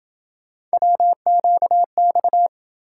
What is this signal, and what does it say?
Morse code, heard as a keyed tone, about 27 wpm, 700 Hz. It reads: WQX